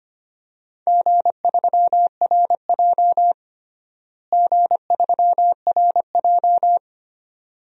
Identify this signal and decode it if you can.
Morse code, heard as a keyed tone, about 25 wpm, 700 Hz. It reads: G3RJ G3RJ
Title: DXCC-100 Call Signs - 4 Characters - Repeated 2x - 25wpm